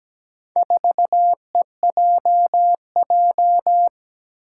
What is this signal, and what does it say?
Morse code, heard as a keyed tone, about 17 wpm, 700 Hz. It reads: 4EJJ
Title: Four Challenging Letter-Number Combinations 17wpm